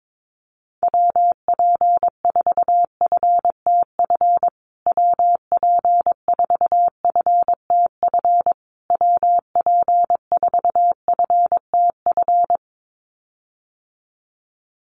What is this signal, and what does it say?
Morse code, heard as a keyed tone, about 22 wpm, 700 Hz. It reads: WP4FTF WP4FTF WP4FTF
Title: USA Call Signs Repeated 3x 22wpm